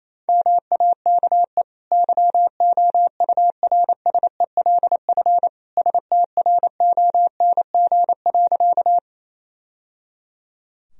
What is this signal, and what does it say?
Morse code, heard as a keyed tone, about 28 wpm, 700 Hz. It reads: MAKE YOURSELF STRONG.
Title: Sentences from Top 1000 Words - Review Part 3 - 28wpm